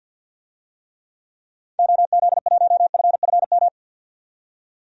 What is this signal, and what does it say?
Morse code, heard as a keyed tone, about 50 wpm, 700 Hz. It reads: OZ1FFM
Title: DXCC-100 Call Signs - 6 Characters - 50wpm